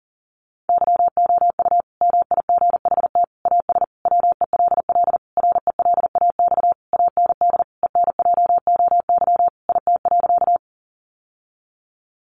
Morse code, heard as a keyed tone, about 40 wpm, 700 Hz. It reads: YOU MIGHT AS WELL RELAX AND ENJOY IT.
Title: Sentences from Top 1400 Words 40wpm